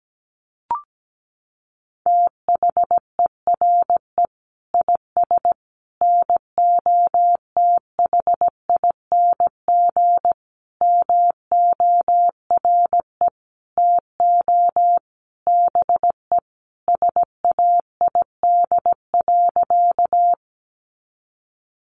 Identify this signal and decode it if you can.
Morse code, heard as a keyed tone, about 17 wpm, 700 Hz. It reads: THERE IS NOTHING MORE TO BE SAID.
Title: Sentences from Top 500 Words 17wpm